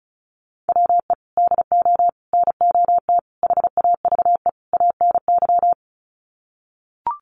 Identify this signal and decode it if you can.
Morse code, heard as a keyed tone, about 34 wpm, 700 Hz. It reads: WE DO NOT HAVE ANY
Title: Sets of 5 Words 35wpm